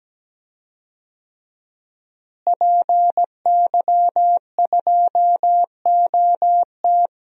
Morse code, heard as a keyed tone, about 17 wpm, 700 Hz. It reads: PY2OT
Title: DXCC-100 Call Signs - 5 Characters - Repeated 3x - 17wpm